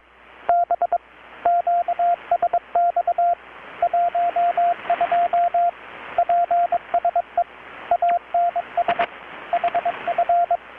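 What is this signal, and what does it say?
Morse code, heard as a keyed tone, about 22 wpm, 690 Hz, with radio noise. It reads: B QSX 12 PSE ANS HF
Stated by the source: Morse from KPH's Night of Nights XX maritime radiotelegraphy event, received in Chiapas, Mexico about 3,800 km away on a Kenwood R-600 with a grounded 10 m long-wire antenna.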